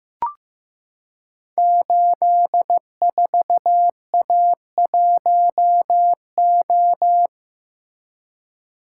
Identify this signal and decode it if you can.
Morse code, heard as a keyed tone, about 15 wpm, 700 Hz. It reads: 84A1O
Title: Code Groups - 5 Letter:Numbers - 15wpm